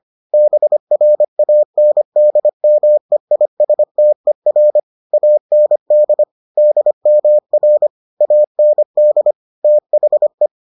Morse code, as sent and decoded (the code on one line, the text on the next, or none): -... .-. .- -. -.. -- . .. ... - . .-. / .- -. -.. / -.. -- .-. / .- -. -.. / - .... .
BRANDMEISTER AND DMR AND THE